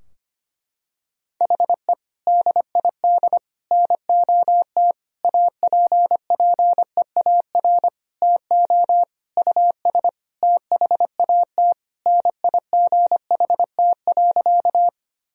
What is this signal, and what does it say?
Morse code, heard as a keyed tone, about 25 wpm, 700 Hz. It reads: HE DID NOT APPEAR TO US THAT NIGHT.
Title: Sentences from Top 1000 Words - Review Part 2 - 25wpm